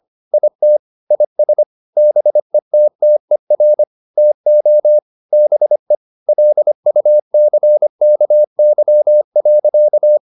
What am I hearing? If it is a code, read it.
Morse code, heard as a keyed tone, about 25 wpm, 600 Hz. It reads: IT IS BETTER TO BE LUCKY.
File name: the old man and the sea part 1 25wpm